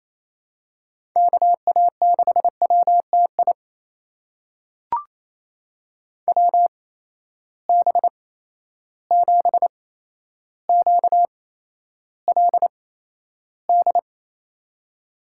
Morse code, heard as a keyed tone, about 28 wpm, spaced as for 12 wpm, 700 Hz. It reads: KA6WTI WB7QLD
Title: USA Call Signs - 6 Characters - 8x Character Spacing - 28wpm